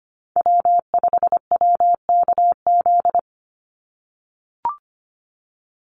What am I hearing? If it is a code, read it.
Morse code, heard as a keyed tone, about 25 wpm, 700 Hz. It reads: W5WKZ
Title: USA Call Signs 25wpm